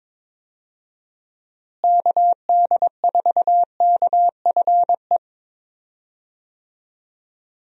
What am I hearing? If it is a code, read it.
Morse code, heard as a keyed tone, about 22 wpm, 700 Hz. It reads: KD4KFE
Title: USA Call Signs - 6 Characters - Repeated 3x - 22wpm